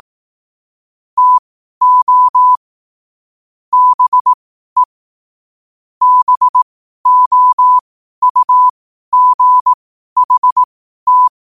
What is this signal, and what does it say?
Morse code, heard as a keyed tone, about 18 wpm, spaced as for 12 wpm, 1000 Hz. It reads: TO BE BOUGHT
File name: qotdfiles / 12wpm